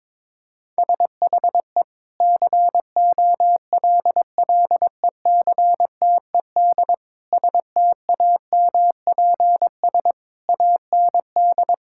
Morse code, heard as a keyed tone, about 22 wpm, 700 Hz. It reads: SHE COLLECTED STAMPS AND
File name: practice / 22wpm / Sentences with 2-Word Phrases - 22wpm